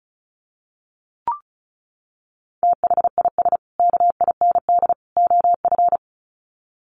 Morse code, heard as a keyed tone, about 34 wpm, 700 Hz. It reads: THIS KIND OF
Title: Sets of 3 Words 35wpm